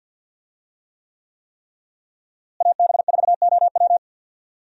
Morse code, heard as a keyed tone, about 52 wpm, 700 Hz. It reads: AB4OW
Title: USA Call Signs - 5 Characters - Repeated 3x - 50wpm